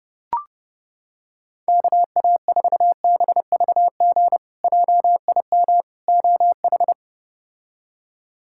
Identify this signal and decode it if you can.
Morse code, heard as a keyed tone, about 30 wpm, 700 Hz. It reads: KA4BVG JIM OH
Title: K1USN SST QSO Practice - 30wpm